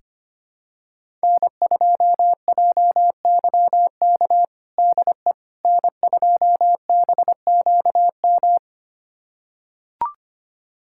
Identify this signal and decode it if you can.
Morse code, heard as a keyed tone, about 25 wpm, 700 Hz. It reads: N2JYK DE N2BQM
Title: USA Callsign DE Callsign - 5 Characters - 25wpm